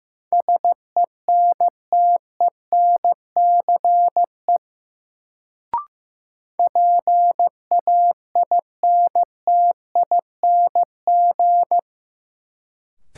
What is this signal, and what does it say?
Morse code, heard as a keyed tone, about 15 wpm, 700 Hz. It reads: SENTENCE PAINTING
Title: Sentences from Top 1200 Words - New Word Drill Alt - 15wpm